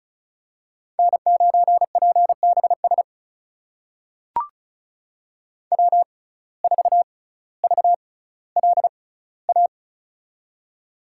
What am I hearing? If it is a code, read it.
Morse code, heard as a keyed tone, about 34 wpm, spaced as for 18 wpm, 700 Hz. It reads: N9PBS W4VLA
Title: USA Call Signs - 5 Characters - 6x Character Spacing - 35wpm